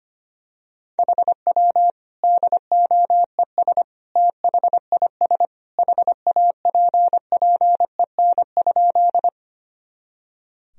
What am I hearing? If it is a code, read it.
Morse code, heard as a keyed tone, about 25 wpm, 700 Hz. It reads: HW DOES THIS HAPPEN?
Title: Sentences from Top 400 Words and Abbreviations 25wpm